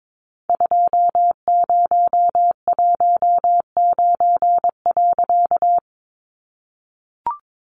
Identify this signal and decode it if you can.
Morse code, heard as a keyed tone, about 22 wpm, 700 Hz. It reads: 2019.
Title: A Sound Of Thunder 22wpm